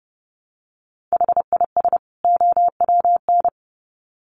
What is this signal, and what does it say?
Morse code, heard as a keyed tone, about 30 wpm, 700 Hz. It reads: HIS OWN
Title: Sets of 2 Words 30wpm